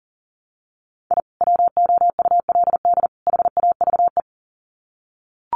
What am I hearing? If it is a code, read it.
Morse code, heard as a keyed tone, about 40 wpm, 700 Hz. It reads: I WOULD HAVE
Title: Sets of 3 Words 40wpm